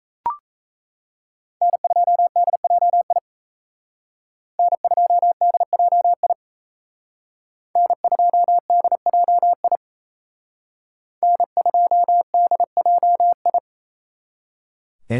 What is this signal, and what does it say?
Morse code, heard as a keyed tone, about 32 wpm, 700 Hz. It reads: N2DJI N2DJI N2DJI N2DJI
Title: Speed Racing - USA Call Signs 28wpm